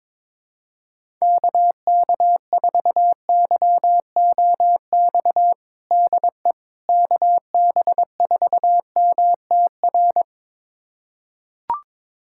Morse code, heard as a keyed tone, about 22 wpm, 700 Hz. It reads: KK4YOX DE KB4MTR
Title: USA Callsign DE Callsign 22wpm